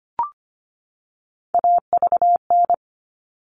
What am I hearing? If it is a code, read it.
Morse code, heard as a keyed tone, about 25 wpm, 700 Hz. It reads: AVN
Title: All Cut Numbers - 3-Digit 25wpm